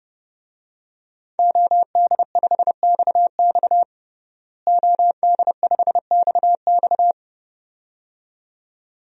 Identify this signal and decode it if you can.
Morse code, heard as a keyed tone, about 30 wpm, 700 Hz. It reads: OD5XX OD5XX
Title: DXCC-100 Call Signs - 5 Characters - Repeated 2x - 30wpm